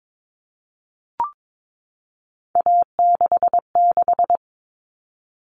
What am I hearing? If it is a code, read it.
Morse code, heard as a keyed tone, about 22 wpm, 700 Hz. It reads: A66
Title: All Cut Numbers - 3-Digit 22wpm